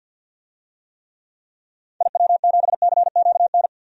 Morse code, heard as a keyed tone, about 51 wpm, 700 Hz. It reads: IW7CQN
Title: DXCC-100 Call Signs - 6 Characters - 50wpm